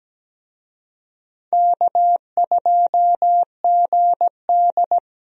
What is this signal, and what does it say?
Morse code, heard as a keyed tone, about 17 wpm, 700 Hz. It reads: K2GD
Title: USA Call Signs - 4 Characters - Repeated 2x - 17wpm